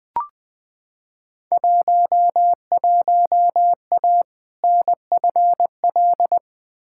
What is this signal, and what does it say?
Morse code, heard as a keyed tone, about 20 wpm, 700 Hz. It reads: 11A NFL
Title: Field Day Exchanges Encore - 20wpm